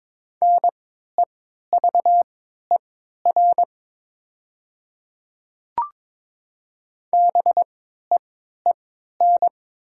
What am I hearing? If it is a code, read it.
Morse code, heard as a keyed tone, about 22 wpm, spaced as for 12 wpm, 690 Hz. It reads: NEVER BEEN
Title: 3x Character Spacing - Top 300 Words - 22wpm